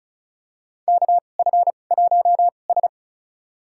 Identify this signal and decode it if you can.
Morse code, heard as a keyed tone, about 36 wpm, spaced as for 25 wpm, 700 Hz. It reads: KF1S
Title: USA Call Signs - 4 Characters - 2x Character Spacing - 35wpm